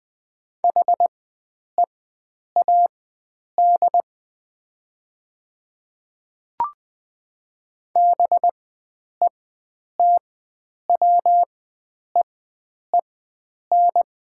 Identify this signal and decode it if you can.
Morse code, heard as a keyed tone, about 20 wpm, spaced as for 9 wpm, 700 Hz. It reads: HEAD BETWEEN
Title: 4x Character Spacing - Top 300 Words - 20wpm